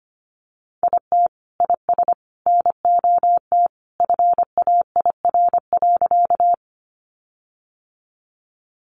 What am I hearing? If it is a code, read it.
Morse code, heard as a keyed tone, about 25 wpm, 700 Hz. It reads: IT IS NOT FAIR.
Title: Sentences from Top 1300 Words 25wpm